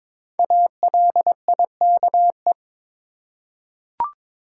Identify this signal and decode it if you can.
Morse code, heard as a keyed tone, about 22 wpm, 700 Hz. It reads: ALIKE
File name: 5 Letter Words - First 3 Letters Given - 22wpm